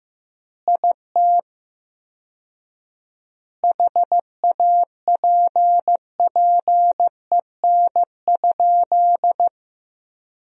Morse code, heard as a keyed tone, about 16 wpm, 700 Hz. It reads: IT HAPPEN?
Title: 4x Word-Spacing - Sentences from Top 400 Words - 15wpm